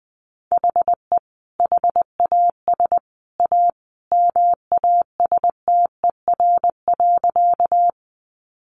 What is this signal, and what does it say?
Morse code, heard as a keyed tone, about 20 wpm, 700 Hz. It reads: HE HAS A MASTER.
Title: Sentences from Top 1300 Words 20wpm